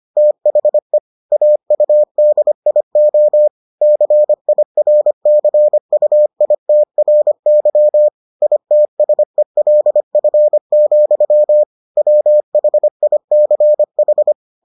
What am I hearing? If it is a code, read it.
Morse code, heard as a keyed tone, about 25 wpm, 600 Hz. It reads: THE AUDIO CIRCUITRY ITSELF, WHICH